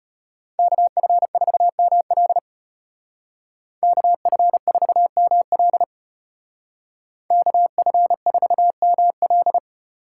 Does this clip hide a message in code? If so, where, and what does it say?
Yes: Morse code from 0.6 to 9.6 s, heard as a keyed tone, about 33 wpm, 700 Hz. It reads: KF4ML KF4ML KF4ML